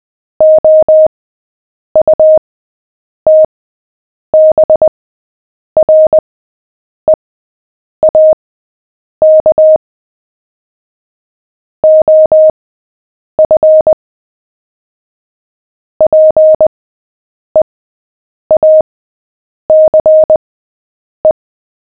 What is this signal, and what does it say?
Morse code, heard as a keyed tone, about 20 wpm, spaced as for 8 wpm, 620 Hz. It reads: OUTBREAK OF PEACE
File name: cw / fists / Short-Story-7-8wpm